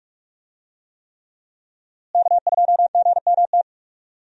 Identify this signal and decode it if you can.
Morse code, heard as a keyed tone, about 42 wpm, 700 Hz. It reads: K2GMT
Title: USA Call Signs - 5 Characters - Repeated 3x - 45wpm